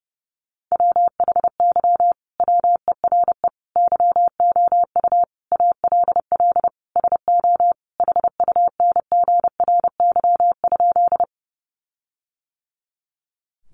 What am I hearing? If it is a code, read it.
Morse code, heard as a keyed tone, about 30 wpm, 700 Hz. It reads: WHY WERE YOU ALL SO HUNGRY?
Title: Sentences from Top 1400 Words 30wpm